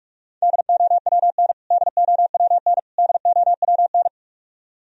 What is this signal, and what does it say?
Morse code, heard as a keyed tone, about 46 wpm, 700 Hz. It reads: DOWN DOWN DOWN
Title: Top 100 Words Repeated 45wpm